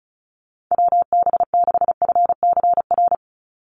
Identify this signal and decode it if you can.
Morse code, heard as a keyed tone, about 34 wpm, 700 Hz. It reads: WB6FCR
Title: USA Call Signs 35wpm